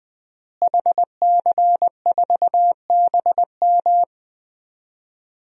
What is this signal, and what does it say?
Morse code, heard as a keyed tone, about 21 wpm, 700 Hz. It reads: HC4BM
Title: DXCC-100 Call Signs - 5 Characters - Repeated 3x - 20wpm